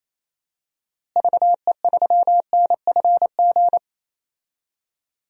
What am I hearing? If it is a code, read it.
Morse code, heard as a keyed tone, about 28 wpm, 700 Hz. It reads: VE3NFG
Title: DXCC-100 Call Signs - 6 Characters - Repeated 2x - 28wpm